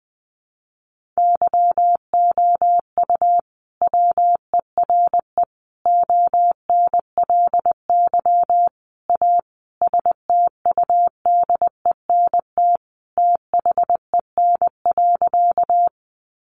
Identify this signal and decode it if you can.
Morse code, heard as a keyed tone, about 20 wpm, 700 Hz. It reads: YOU WERE ONLY A STUDENT THEN.
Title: Sentences from Top 1000 Words 20wpm